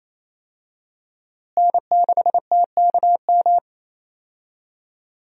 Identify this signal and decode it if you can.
Morse code, heard as a keyed tone, about 28 wpm, 700 Hz. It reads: N6TKM
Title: USA Call Signs - 5 Characters - Repeated 2x - 28wpm